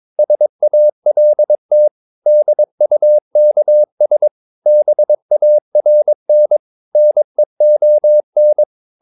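Morse code, heard as a keyed tone, about 22 wpm, 600 Hz. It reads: SALT DUKS BARN NEON
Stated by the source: Morse practice of short Danish words at 22 wpm, from CW med Gnister.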